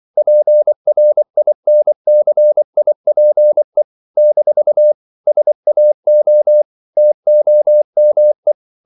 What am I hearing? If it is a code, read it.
Morse code, heard as a keyed tone, about 24 wpm, 600 Hz. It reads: PRINCIPE - SAO TOME
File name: CW / Practice / NationalCapitals24